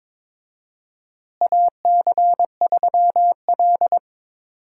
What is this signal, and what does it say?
Morse code, heard as a keyed tone, about 22 wpm, 700 Hz. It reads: AC3L
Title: USA Call Signs - 4 Characters - Repeated 3x - 22wpm